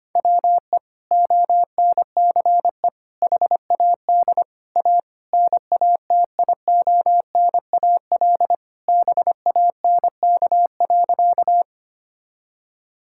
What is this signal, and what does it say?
Morse code, heard as a keyed tone, about 25 wpm, 700 Hz. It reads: WE ONCE HAD A NATIONAL BANK.
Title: Sentences from Top 1000 Words - Review Part 6 - 25wpm